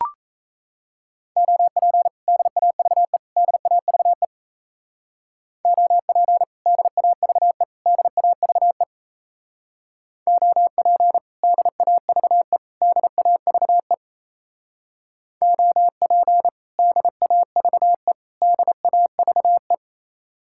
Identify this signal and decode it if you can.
Morse code, heard as a keyed tone, about 36 wpm, 700 Hz. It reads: OP DAVE DAVE OP DAVE DAVE OP DAVE DAVE OP DAVE DAVE